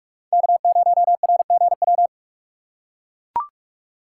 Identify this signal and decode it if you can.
Morse code, heard as a keyed tone, about 43 wpm, 700 Hz. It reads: K0RGW
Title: USA Call Signs - 5 Characters - 6x Character Spacing - 45wpm